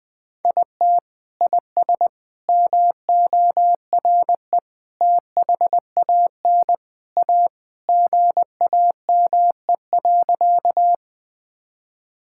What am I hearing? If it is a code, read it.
Morse code, heard as a keyed tone, about 20 wpm, 700 Hz. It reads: IT IS MORE THAN A GAME.
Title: Sentences with 3-Word Phrases - Spoken Beginning - 20wpm